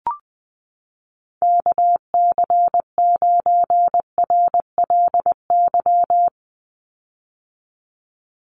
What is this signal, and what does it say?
Morse code, heard as a keyed tone, about 20 wpm, 700 Hz. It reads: KC9RLY
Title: USA Call Signs 20wpm